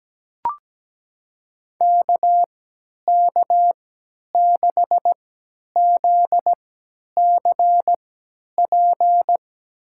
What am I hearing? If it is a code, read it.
Morse code, heard as a keyed tone, about 17 wpm, spaced as for 10 wpm, 700 Hz. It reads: KK6ZCP
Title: USA Call Signs - 3x Character Spacing - 17wpm